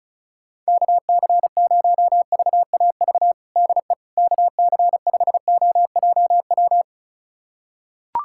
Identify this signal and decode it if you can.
Morse code, heard as a keyed tone, about 36 wpm, 700 Hz. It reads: KC0VAV DE KC5OJW
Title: USA Callsign DE Callsign 35wpm